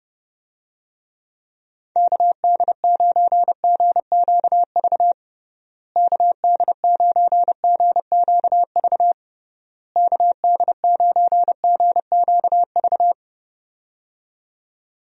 Morse code, heard as a keyed tone, about 30 wpm, 700 Hz. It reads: KD9GQV KD9GQV KD9GQV
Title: USA Call Signs - 6 Characters - Repeated 3x - 30wpm